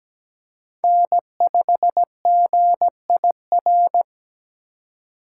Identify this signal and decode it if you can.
Morse code, heard as a keyed tone, about 20 wpm, 700 Hz. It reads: N5GIR
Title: Speed Racing - USA 1x3 Call Signs 17wpm